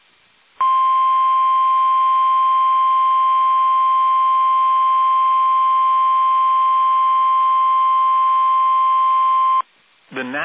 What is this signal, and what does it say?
Emergency Alert System (SAME) tones (the 1050 Hz attention tone). It is heard from 0.5 to 9.7 s.